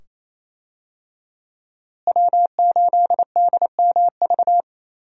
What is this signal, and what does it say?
Morse code, heard as a keyed tone, about 28 wpm, 700 Hz. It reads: W8DMV
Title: USA Call Signs - 5 Characters - Repeated 2x - 28wpm